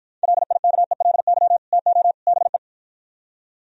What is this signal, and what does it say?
Morse code, heard as a keyed tone, about 53 wpm, 700 Hz. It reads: LIKELY TO BE